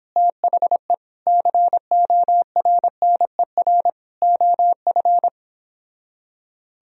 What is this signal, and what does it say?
Morse code, heard as a keyed tone, about 31 wpm, 700 Hz. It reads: THE CORNER OF